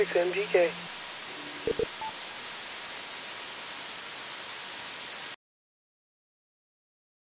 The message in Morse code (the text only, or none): none